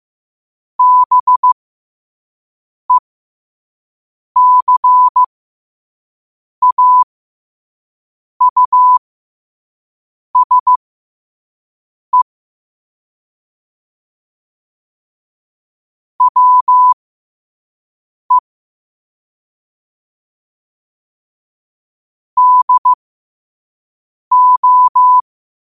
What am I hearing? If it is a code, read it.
Morse code, heard as a keyed tone, about 15 wpm, spaced as for 5 wpm, 1000 Hz. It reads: BECAUSE WE DO
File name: qotdfiles / 5wpm